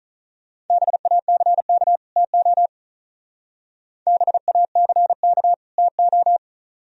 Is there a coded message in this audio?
Yes: Morse code from 0.7 to 6.4 s, heard as a keyed tone, about 44 wpm, 690 Hz. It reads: BACK TO BACK TO